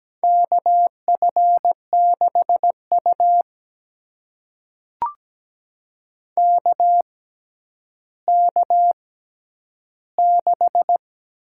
Morse code, heard as a keyed tone, about 17 wpm, spaced as for 10 wpm, 700 Hz. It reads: KF6U KK6
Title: USA Call Signs - 6x Character Spacing - 17wpm